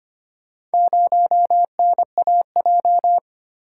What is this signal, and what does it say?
Morse code, heard as a keyed tone, about 25 wpm, 700 Hz. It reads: 0NAJ
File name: practice / 25wpm / Four Letter-Number Combinations - 25wpm